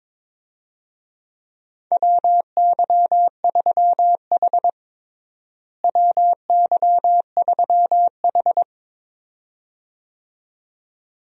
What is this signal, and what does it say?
Morse code, heard as a keyed tone, about 22 wpm, 700 Hz. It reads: WY3H WY3H
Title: USA Call Signs - 4 Characters - Repeated 2x - 22wpm